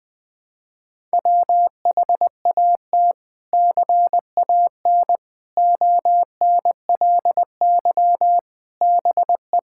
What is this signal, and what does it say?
Morse code, heard as a keyed tone, about 21 wpm, 700 Hz. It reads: WHAT CAN ONLY BE
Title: Sets of 6 Words - 20wpm